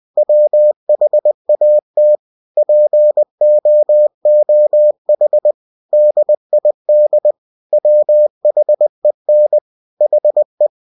Morse code, heard as a keyed tone, about 20 wpm, 600 Hz. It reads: WHAT POOH DID WHEN HE
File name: cw-books / winnie-the-pooh / winnie-20wpm-page-0102